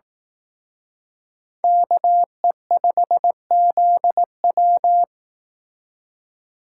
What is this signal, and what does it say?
Morse code, heard as a keyed tone, about 18 wpm, 700 Hz. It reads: KE5ZW
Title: Speed Racing - USA 2x2 Call Signs 15wpm